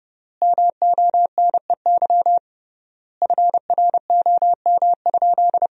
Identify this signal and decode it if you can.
Morse code, heard as a keyed tone, about 30 wpm, 700 Hz. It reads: MONEY FROM?
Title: 3x Word-Spacing - Sentences from Top 400 Words - 30wpm